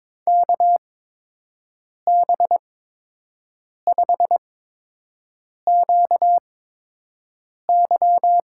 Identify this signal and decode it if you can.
Morse code, heard as a keyed tone, about 22 wpm, spaced as for 6 wpm, 700 Hz. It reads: KB5QY
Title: USA Call Signs - 6 Characters - 8x Character Spacing - 22wpm